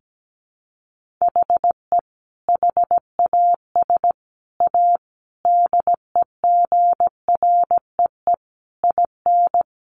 Morse code, heard as a keyed tone, about 17 wpm, 700 Hz. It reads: HE HAS A DEGREE IN
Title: Sentences from Top 1500 Words 17wpm